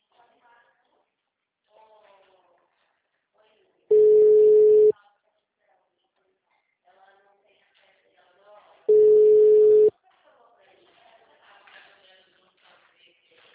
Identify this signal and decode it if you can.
Ringback tone (Europe, 425 Hz), with speech.